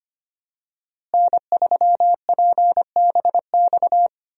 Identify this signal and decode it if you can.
Morse code, heard as a keyed tone, about 25 wpm, 700 Hz. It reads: N3PBX
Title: USA Call Signs - 5 Characters - Repeated 2x - 25wpm